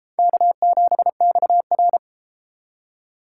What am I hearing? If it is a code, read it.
Morse code, heard as a keyed tone, about 32 wpm, 700 Hz. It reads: K7XR